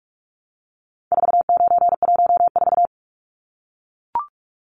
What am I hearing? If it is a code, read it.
Morse code, heard as a keyed tone, about 44 wpm, 700 Hz. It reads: 4914
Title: Numbers - 4-Digit 45wpm